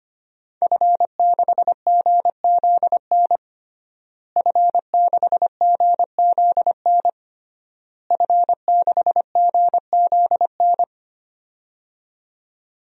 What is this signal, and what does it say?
Morse code, heard as a keyed tone, about 25 wpm, 700 Hz. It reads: F6GZN F6GZN F6GZN
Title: DXCC-100 Call Signs - 5 Characters - Repeated 3x - 25wpm